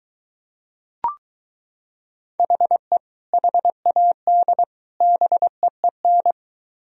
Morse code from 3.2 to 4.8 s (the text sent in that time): HAD